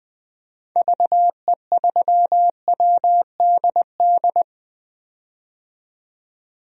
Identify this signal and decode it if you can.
Morse code, heard as a keyed tone, about 20 wpm, 700 Hz. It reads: VE3WDD
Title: DXCC-100 Call Signs - 6 Characters - 20wpm